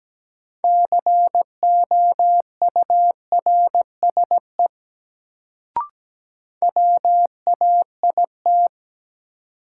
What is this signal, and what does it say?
Morse code, heard as a keyed tone, about 17 wpm, 700 Hz. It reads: COURSE WAIT
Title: Sentences from Top 500 Words - New Word Drill - 17wpm